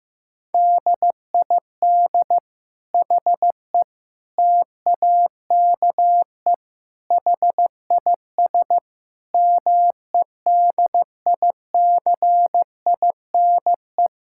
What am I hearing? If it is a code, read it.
Morse code, heard as a keyed tone, about 15 wpm, 700 Hz. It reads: DID HE TAKE HIS MEDICINE
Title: Sentences from Top 1000 Words - Review Part 6 - 15wpm